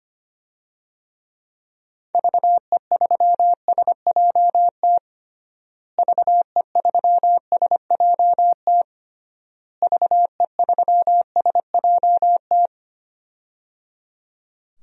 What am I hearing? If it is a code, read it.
Morse code, heard as a keyed tone, about 25 wpm, 700 Hz. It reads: VE3SJT VE3SJT VE3SJT
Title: DXCC-100 Call Signs - 6 Characters - Repeated 3x - 25wpm